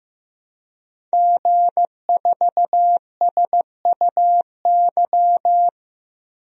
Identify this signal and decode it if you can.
Morse code, heard as a keyed tone, about 15 wpm, 700 Hz. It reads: G4SUY
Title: DXCC-100 Call Signs - 5 Characters - 15wpm